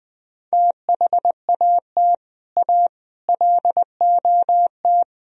Morse code, heard as a keyed tone, about 20 wpm, 700 Hz. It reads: THAT A LOT
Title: Speed-Racing - Sets of 5 Words 20wpm